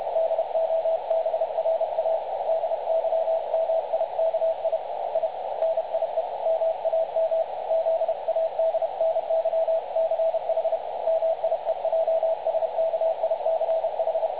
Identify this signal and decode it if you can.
Morse code, heard as a keyed tone, about 30 wpm, 660 Hz, with radio noise. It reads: DOBRA DYMIKI I NIE MAM ZADNYCH MIEJSKICH
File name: dlazaawansowanych / nagraniazpasma / websdr recording 3565.0kHz